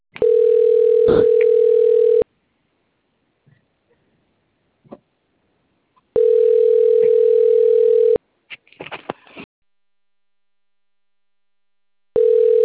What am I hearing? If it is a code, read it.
Ringback tone (North America), 440 + 480 Hz.
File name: out- - - - - .68053